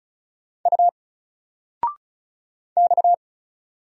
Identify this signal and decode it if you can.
Morse code, heard as a keyed tone, about 36 wpm, 700 Hz. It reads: U X
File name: Single-Letter-Number-35wpm